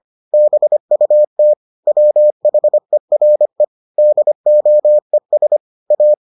Morse code, heard as a keyed tone, about 25 wpm, 600 Hz. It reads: BUT WHERE DOES A